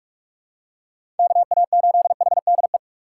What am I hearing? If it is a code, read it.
Morse code, heard as a keyed tone, about 46 wpm, 700 Hz. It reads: KA8HDE
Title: K1USN SST QSO Practice - 45wpm